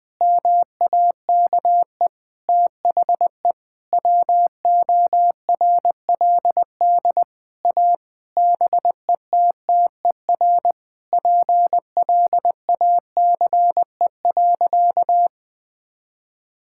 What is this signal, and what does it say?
Morse code, heard as a keyed tone, about 21 wpm, 700 Hz. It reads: MAKE THE WORLD A BETTER PLACE.
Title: Sentences with 6-Word Phrases - 20wpm